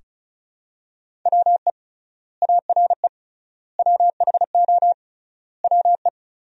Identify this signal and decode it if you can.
Morse code, heard as a keyed tone, about 36 wpm, 700 Hz. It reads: WE ARE WHO WE
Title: 3x Word-Spacing - Sentences from Top 200 Words - 35wpm